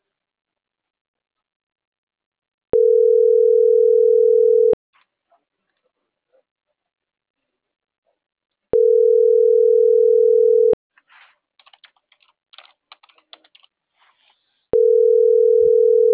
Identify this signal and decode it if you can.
Ringback tone (North America), 440 + 480 Hz.